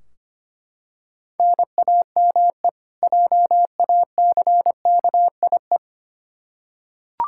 Morse code, heard as a keyed tone, about 25 wpm, 700 Hz. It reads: NAME JACKIE
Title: K1USN SST QSO Practice 25wpm